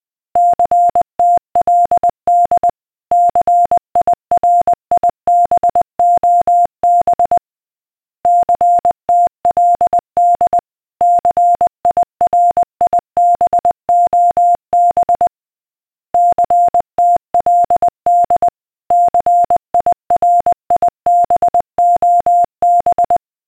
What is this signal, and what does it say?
Morse code, heard as a keyed tone, about 20 wpm, 700 Hz. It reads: CTLD CIRIBOB CTLD CIRIBOB CTLD CIRIBOB
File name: DCS-CSAR / beacon